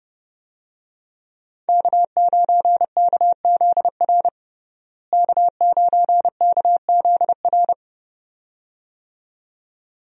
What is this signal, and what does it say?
Morse code, heard as a keyed tone, about 30 wpm, 700 Hz. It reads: K9KZR K9KZR
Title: USA Call Signs - 5 Characters - Repeated 2x - 30wpm